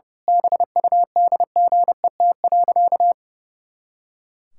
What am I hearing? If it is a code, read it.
Morse code, heard as a keyed tone, about 31 wpm, 700 Hz. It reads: BUDGET.
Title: Pangram Challenge v2 - 30wpm